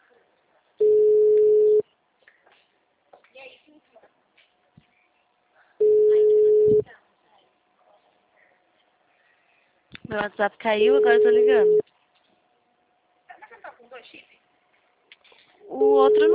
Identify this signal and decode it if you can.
Ringback tone (Europe, 425 Hz), with speech.